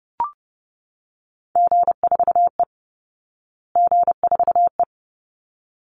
Morse code, heard as a keyed tone, about 30 wpm, 700 Hz. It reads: G4E G4E
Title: Call Signs Repeated 3x - 30wpm